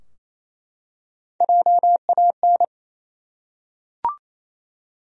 Morse code, heard as a keyed tone, about 28 wpm, 700 Hz. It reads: JAN
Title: USA Common Names - Max 3 Letters - 28wpm